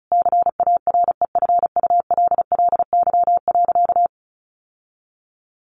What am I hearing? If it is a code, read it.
Morse code, heard as a keyed tone, about 34 wpm, 700 Hz. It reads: CAREFULLY.
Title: Sentences from Top 1300 Words 35wpm